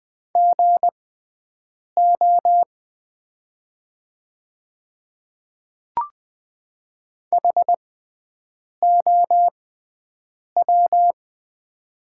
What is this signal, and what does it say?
Morse code, heard as a keyed tone, about 20 wpm, spaced as for 7 wpm, 700 Hz. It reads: GO HOW
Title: 6x Character Spacing - Top 100 Words - 20wpm